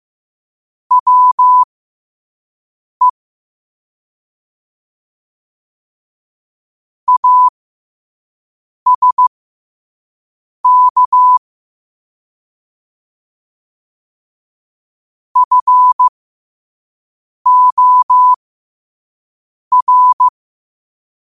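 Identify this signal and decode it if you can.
Morse code, heard as a keyed tone, about 15 wpm, spaced as for 5 wpm, 1000 Hz. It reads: WE ASK FOR
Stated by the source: 5 wpm Morse code practice recording.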